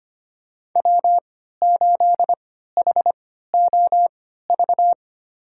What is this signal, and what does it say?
Morse code, heard as a keyed tone, about 25 wpm, spaced as for 14 wpm, 700 Hz. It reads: W8HOV
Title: USA Call Signs - 5 Characters - 3x Character Spacing - 25wpm